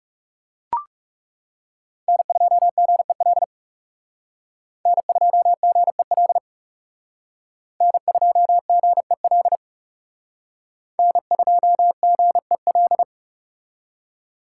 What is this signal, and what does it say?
Morse code, heard as a keyed tone, about 39 wpm, 700 Hz. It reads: N2GEL N2GEL N2GEL N2GEL